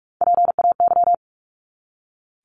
Morse code, heard as a keyed tone, about 44 wpm, 700 Hz. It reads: PAY
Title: Top 501-750 Words 45wpm